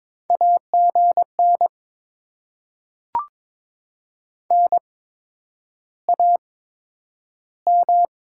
Morse code, heard as a keyed tone, about 22 wpm, spaced as for 9 wpm, 700 Hz. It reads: AGN NAM
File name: Top 20 QSO Elements - 8x Character Spacing - 22wpm